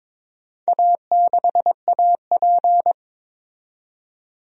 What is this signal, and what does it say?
Morse code, heard as a keyed tone, about 22 wpm, 700 Hz. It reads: A6AP
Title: DXCC-100 Call Signs - 4 Characters - 22wpm